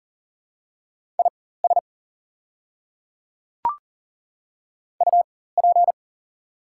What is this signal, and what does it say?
Morse code, heard as a keyed tone, about 38 wpm, spaced as for 18 wpm, 700 Hz. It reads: IS UP
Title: 4x Character Spacing - Top 100 Words - 40wpm